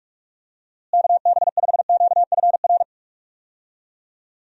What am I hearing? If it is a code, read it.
Morse code, heard as a keyed tone, about 47 wpm, 700 Hz. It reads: KB5QFR
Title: USA Call Signs - 6 Characters - Repeated 3x - 45wpm